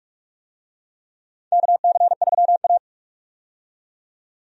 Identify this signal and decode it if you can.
Morse code, heard as a keyed tone, about 47 wpm, 700 Hz. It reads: KC3A
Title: USA Call Signs - 4 Characters - 45wpm